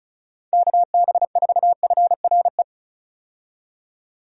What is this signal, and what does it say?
Morse code, heard as a keyed tone, about 36 wpm, 700 Hz. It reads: KB4FRE
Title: USA Call Signs - 6 Characters - 35wpm